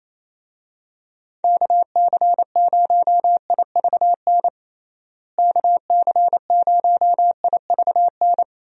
Morse code, heard as a keyed tone, about 27 wpm, 700 Hz. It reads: KC0IVN KC0IVN
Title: USA Call Signs - 6 Characters - Repeated 2x - 28wpm